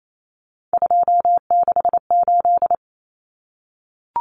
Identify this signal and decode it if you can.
Morse code, heard as a keyed tone, about 28 wpm, 700 Hz. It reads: 268
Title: Common Cut Numbers - 3-Digit 28wpm